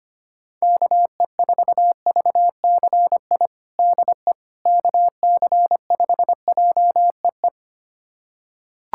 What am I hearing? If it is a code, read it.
Morse code, heard as a keyed tone, about 25 wpm, 700 Hz. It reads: KE4VCI DE KC5JEE
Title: USA Callsign DE Callsign - 6 Characters - 25wpm